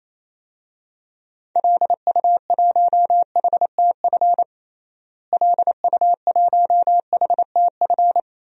Morse code, heard as a keyed tone, about 28 wpm, 700 Hz. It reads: LU1HTF LU1HTF
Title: DXCC-100 Call Signs - 6 Characters - Repeated 2x - 28wpm